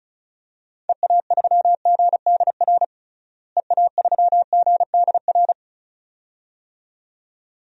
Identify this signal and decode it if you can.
Morse code, heard as a keyed tone, about 36 wpm, 700 Hz. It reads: EA3GDR EA3GDR
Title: DXCC-100 Call Signs - 6 Characters - Repeated 2x - 35wpm